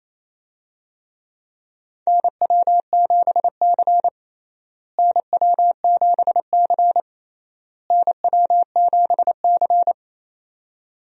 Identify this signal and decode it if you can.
Morse code, heard as a keyed tone, about 28 wpm, 700 Hz. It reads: NW7C NW7C NW7C
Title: USA Call Signs - 4 Characters - Repeated 3x - 28wpm